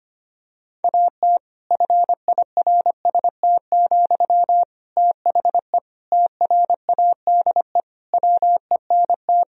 Morse code, heard as a keyed tone, about 25 wpm, 700 Hz. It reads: AT FIRST, THE TRADE WENT
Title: Sentences from Top 1000 Words - Review Part 4 - 25wpm